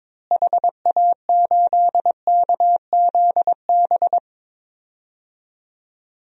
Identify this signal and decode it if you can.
Morse code, heard as a keyed tone, about 22 wpm, 700 Hz. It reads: HA8KZB
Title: DXCC-100 Call Signs - 6 Characters - Repeated 3x - 22wpm